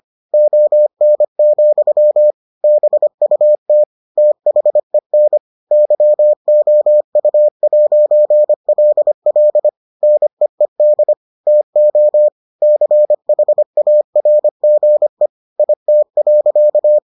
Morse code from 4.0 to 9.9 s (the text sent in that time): THEN YOU'LL